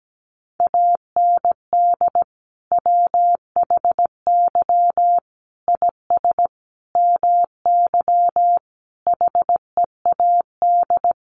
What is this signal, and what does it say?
Morse code, heard as a keyed tone, about 17 wpm, 700 Hz. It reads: AND WHY IS MY HEAD
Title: Sentences from Top 1100 Words 17wpm